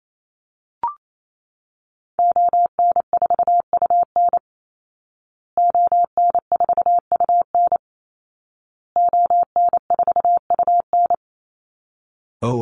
Morse code, heard as a keyed tone, about 28 wpm, 700 Hz. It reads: ON4UN ON4UN ON4UN
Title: Call Signs Repeated 2x - 28wpm